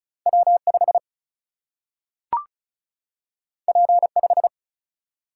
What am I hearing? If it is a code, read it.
Morse code, heard as a keyed tone, about 36 wpm, 700 Hz. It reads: W5 P5
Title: Single-Letter Single-Number 35wpm